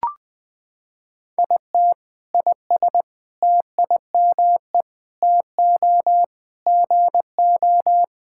Morse code, heard as a keyed tone, about 20 wpm, 700 Hz. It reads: IT IS TIME TO GO
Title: Sentences with 7-Word Phrases - 20wpm